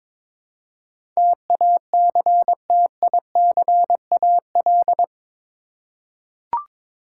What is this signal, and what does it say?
Morse code, heard as a keyed tone, about 22 wpm, 700 Hz. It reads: TACTICAL
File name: practice / 22wpm / Eight Letter Words - 22wpm